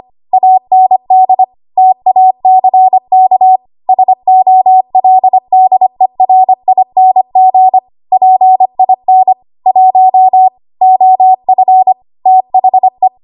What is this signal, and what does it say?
Morse code, heard as a keyed tone, about 25 wpm, 750 Hz. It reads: AND TACK SOLDERING PIN 1 OF THE
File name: Archive / 25 WPM / 25WPM